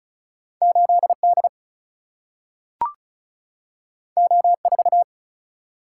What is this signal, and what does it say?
Morse code, heard as a keyed tone, about 34 wpm, 700 Hz. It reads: 8D O4
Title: Single-Letter Single-Number 35wpm